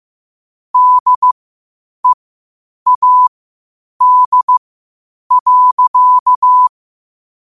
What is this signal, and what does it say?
Morse code, heard as a keyed tone, about 15 wpm, spaced as for 8 wpm, 1000 Hz. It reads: DEAD.
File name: qotdfiles / 8wpm